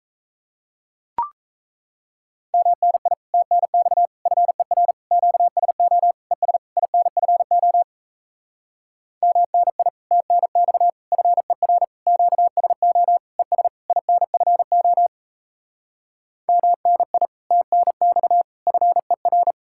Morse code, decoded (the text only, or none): MNI TNX FER QSO ES INFO MNI TNX FER QSO ES INFO MNI TNX FER